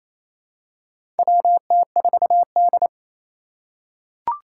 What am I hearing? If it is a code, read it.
Morse code, heard as a keyed tone, about 28 wpm, 700 Hz. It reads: WT4D
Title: USA Call Signs - 4 Characters - 2x Character Spacing - 28wpm